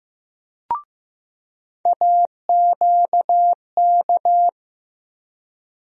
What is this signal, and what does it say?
Morse code, heard as a keyed tone, about 15 wpm, 700 Hz. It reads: AQK
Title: USA Call Sign Postfixes - 3 Letters 15wpm